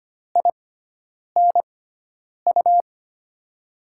Morse code, heard as a keyed tone, about 25 wpm, spaced as for 9 wpm, 700 Hz. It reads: INU
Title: USA Call Signs - 6x Character Spacing - 25wpm